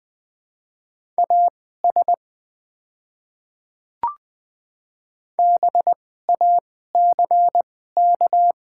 Morse code, heard as a keyed tone, about 20 wpm, spaced as for 14 wpm, 700 Hz. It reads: AS BACK
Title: 2x Character Spacing - Top 300 Words - 20wpm